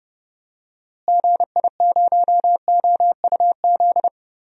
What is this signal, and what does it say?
Morse code, heard as a keyed tone, about 30 wpm, 700 Hz. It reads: GI0OUZ
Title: DXCC-100 Call Signs - 6 Characters - Repeated 3x - 30wpm